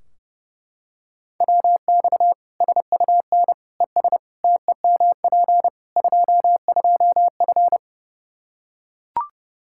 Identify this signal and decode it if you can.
Morse code, heard as a keyed tone, about 30 wpm, 700 Hz. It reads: WX SUN ES TEMP 22F
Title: Practice QSO - Exchange 2 - Info - 30wpm